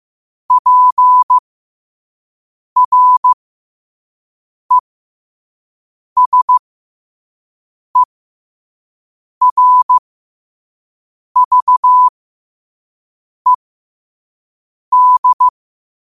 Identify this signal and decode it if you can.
Morse code, heard as a keyed tone, about 15 wpm, spaced as for 5 wpm, 1000 Hz. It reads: PRESERVED
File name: qotdfiles / 5wpm